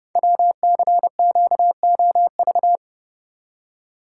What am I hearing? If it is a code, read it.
Morse code, heard as a keyed tone, about 30 wpm, 700 Hz. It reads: WCQOV
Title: Code Groups - 5 Letters - 30wpm